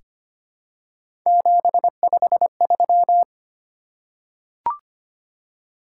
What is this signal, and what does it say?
Morse code, heard as a keyed tone, about 25 wpm, 700 Hz. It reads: 753
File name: Three Letter-Number Combinations - 25wpm